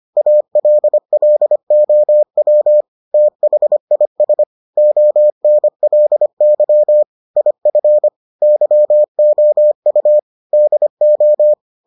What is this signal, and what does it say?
Morse code, heard as a keyed tone, about 25 wpm, 600 Hz. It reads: ALLOW THIS ONLY IF YOU DO